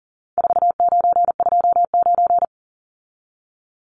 Morse code, heard as a keyed tone, about 40 wpm, 700 Hz. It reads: 4929
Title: Numbers - 4-Digit 40wpm